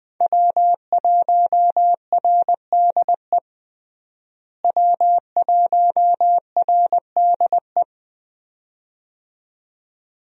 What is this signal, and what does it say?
Morse code, heard as a keyed tone, about 20 wpm, 700 Hz. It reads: W1RDE W1RDE
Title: USA Call Signs - 5 Characters - Repeated 2x - 20wpm